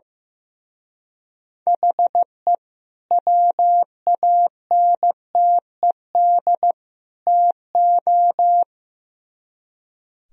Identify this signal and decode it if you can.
Morse code, heard as a keyed tone, about 16 wpm, 700 Hz. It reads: HE WANTED TO
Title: Speed-Racing - Sets of 3 Words 15wpm